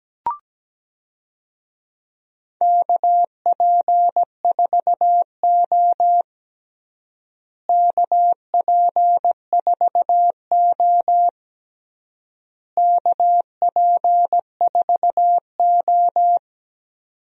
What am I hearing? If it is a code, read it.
Morse code, heard as a keyed tone, about 17 wpm, 700 Hz. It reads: KP4O KP4O KP4O
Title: DXCC-100 Call Signs - 4 Characters - Repeated 3x - 17wpm